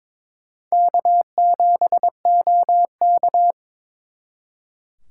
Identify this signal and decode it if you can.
Morse code, heard as a keyed tone, about 26 wpm, 700 Hz. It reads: K7OK
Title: Speed Racing - USA 1x2 Call Signs 22wpm